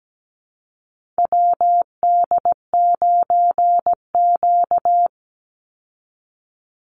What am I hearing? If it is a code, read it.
Morse code, heard as a keyed tone, about 17 wpm, 700 Hz. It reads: WD9Q
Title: Call Signs Repeated 3x - 17wpm